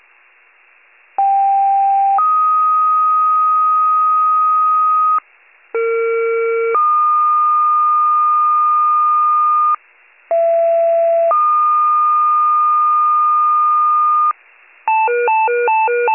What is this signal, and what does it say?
Two-tone page, heard as a one-second tone, then a three-second tone, about 790 then 1250 Hz, about 460 then 1220 Hz, about 670 then 1220 Hz. It is heard from 1.2 to 14.3 s.
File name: pager / 39 RESCUE / wcalert